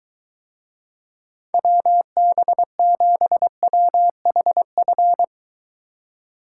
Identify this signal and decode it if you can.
Morse code, heard as a keyed tone, about 23 wpm, 700 Hz. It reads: WB7WHF